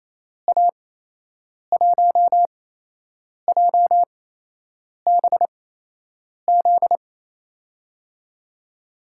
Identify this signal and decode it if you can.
Morse code, heard as a keyed tone, about 28 wpm, spaced as for 8 wpm, 700 Hz. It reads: A1JBZ
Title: USA Call Signs - 6 Characters - 8x Character Spacing - 28wpm